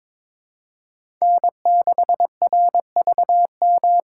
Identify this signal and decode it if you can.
Morse code, heard as a keyed tone, about 22 wpm, 700 Hz. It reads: N6RVM
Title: USA Call Signs - 5 Characters - 3x Character Spacing - 22wpm